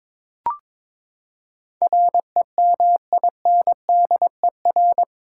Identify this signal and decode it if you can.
Morse code, heard as a keyed tone, about 21 wpm, 700 Hz. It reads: REMINDER
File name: practice / 22wpm / Eight Letter Words - 22wpm